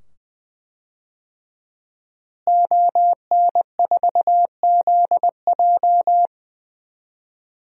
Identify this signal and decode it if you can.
Morse code, heard as a keyed tone, about 20 wpm, 700 Hz. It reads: ON4ZJ
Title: DXCC-100 Call Signs - 5 Characters - Repeated 2x - 20wpm